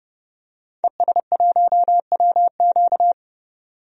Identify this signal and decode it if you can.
Morse code, heard as a keyed tone, about 30 wpm, 700 Hz. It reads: ES1WQ
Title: DXCC-100 Call Signs - 5 Characters - 30wpm